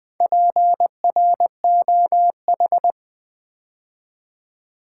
Morse code, heard as a keyed tone, about 20 wpm, 700 Hz. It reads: PROH
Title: Build A Word - 2 Letters At A Time - 8-Letter Words - 20wpm